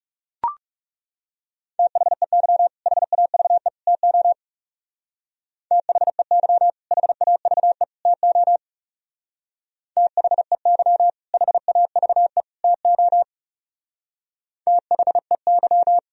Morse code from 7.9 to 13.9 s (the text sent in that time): TO THEY HAVE TO